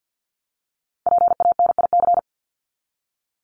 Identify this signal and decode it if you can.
Morse code, heard as a keyed tone, about 50 wpm, 700 Hz. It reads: PANIC
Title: Top 1751-2000 Words 50wpm